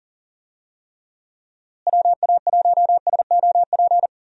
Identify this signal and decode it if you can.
Morse code, heard as a keyed tone, about 40 wpm, 700 Hz. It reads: WA1SOP
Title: USA Call Signs - 6 Characters - Repeated 2x - 40wpm